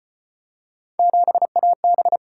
Morse code, heard as a keyed tone, about 35 wpm, 700 Hz. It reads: 7AB